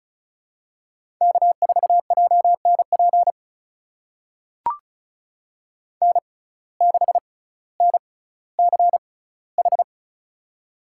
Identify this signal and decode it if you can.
Morse code, heard as a keyed tone, about 36 wpm, spaced as for 18 wpm, 700 Hz. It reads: K4JNP N6NCH
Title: USA Call Signs - 6x Character Spacing - 35wpm